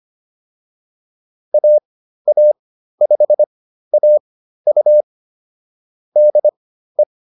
Morse code, heard as a keyed tone, about 25 wpm, spaced as for 13 wpm, 600 Hz. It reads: AA5AU DE
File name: QSO104 13